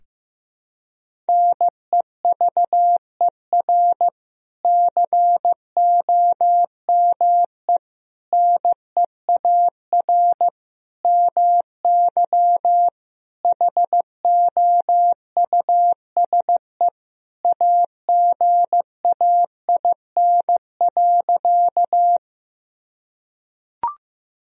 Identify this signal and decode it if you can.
Morse code, heard as a keyed tone, about 15 wpm, 700 Hz. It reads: NEVER COME NEAR MY HOUSE AGAIN.
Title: Sentences from Top 1000 Words - Review Part 1 - 15wpm